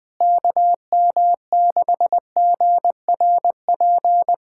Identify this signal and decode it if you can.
Morse code, heard as a keyed tone, about 21 wpm, 700 Hz. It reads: KM6GRP
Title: USA Callsign DE Callsign - 6 Characters - 20wpm